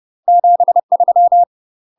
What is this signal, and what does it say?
Morse code, heard as a keyed tone, about 30 wpm, 700 Hz. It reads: 73
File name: UK MASTER QSOs TRAINING/30WPM